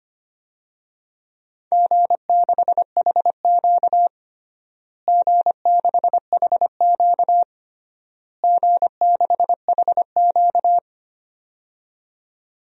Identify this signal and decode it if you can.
Morse code, heard as a keyed tone, about 25 wpm, 700 Hz. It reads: G6HQ G6HQ G6HQ
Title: DXCC-100 Call Signs - 4 Characters - Repeated 3x - 25wpm